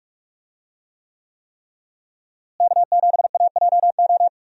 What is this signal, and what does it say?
Morse code, heard as a keyed tone, about 45 wpm, 700 Hz. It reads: K7AJO
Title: USA Call Signs - 5 Characters - Repeated 3x - 45wpm